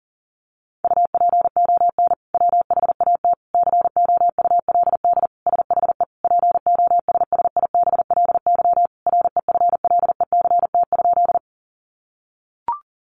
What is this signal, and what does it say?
Morse code, heard as a keyed tone, about 40 wpm, 700 Hz. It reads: UPON WHAT COULD SHE POSSIBLY REFLECT?
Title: Sentences from Top 1500 Words 40wpm